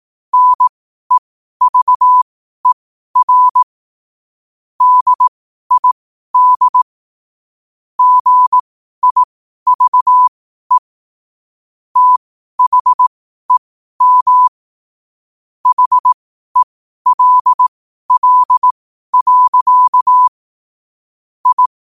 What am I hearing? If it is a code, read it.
Morse code, heard as a keyed tone, about 18 wpm, spaced as for 12 wpm, 1000 Hz. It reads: NEVER DID GIVE THEM HELL. I
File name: qotdfiles / 12wpm